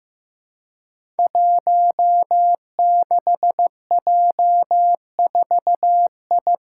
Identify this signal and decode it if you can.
Morse code, heard as a keyed tone, about 15 wpm, 700 Hz. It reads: 16J4I
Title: Five Challenging Letter-Number Combinations 15wpm